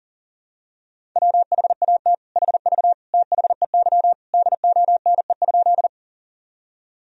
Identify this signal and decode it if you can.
Morse code, heard as a keyed tone, about 40 wpm, 700 Hz. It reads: WHAT HV THEY DONE?
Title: Sentences from Top 500 Words and Abbreviations 40wpm